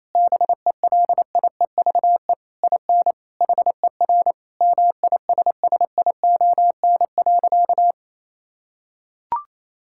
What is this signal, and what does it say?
Morse code, heard as a keyed tone, about 28 wpm, 700 Hz. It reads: BELIEVE IN HER MISSION.
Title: Sentences from Top 1600 Words 28wpm